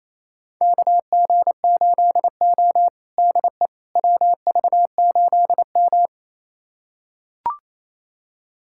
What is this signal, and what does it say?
Morse code, heard as a keyed tone, about 28 wpm, 700 Hz. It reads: KG8O DE WV8M
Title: USA Callsign DE Callsign - 4 Characters - 28wpm